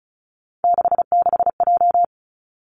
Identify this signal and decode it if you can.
Morse code, heard as a keyed tone, about 36 wpm, 700 Hz. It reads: 66J
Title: Three Challenging Letter-Number Combinations 35wpm